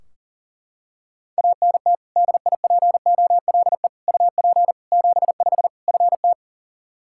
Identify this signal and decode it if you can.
Morse code, heard as a keyed tone, about 40 wpm, 700 Hz. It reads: ANT DIPOLE UP 75 FT
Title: K1USN SST QSO Practice 40wpm